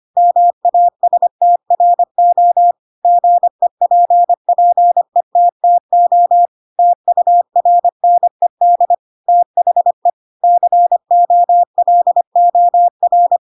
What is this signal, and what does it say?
Morse code, heard as a keyed tone, about 25 wpm, 700 Hz. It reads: MASTRO GEPPETTO TURNED THE COLOR
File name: Pinocchio / Pinocchio ch2 25 25 ik1hge